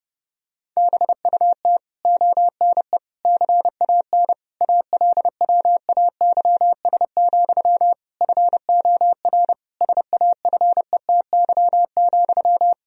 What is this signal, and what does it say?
Morse code, heard as a keyed tone, about 32 wpm, 700 Hz. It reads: BUT ONE CAN ALWAYS, FOR SAFETY,
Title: Sentences from Top 1000 Words - Review Part 4 - 30wpm